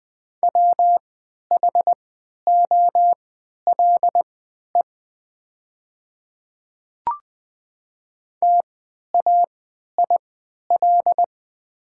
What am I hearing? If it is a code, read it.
Morse code, heard as a keyed tone, about 20 wpm, spaced as for 11 wpm, 700 Hz. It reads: WHOLE TAIL
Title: 3x Character Spacing - Top 500 Words - 20wpm